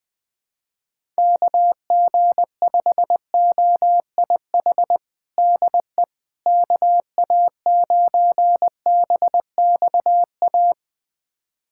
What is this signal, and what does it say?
Morse code, heard as a keyed tone, about 21 wpm, 700 Hz. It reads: KG5OIH DE KA9BXA
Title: USA Callsign DE Callsign - 6 Characters - 20wpm